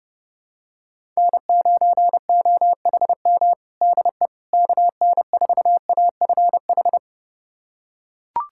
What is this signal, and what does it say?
Morse code, heard as a keyed tone, about 30 wpm, 700 Hz. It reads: N9OHM DE KN4AFH
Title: USA Callsign DE Callsign 30wpm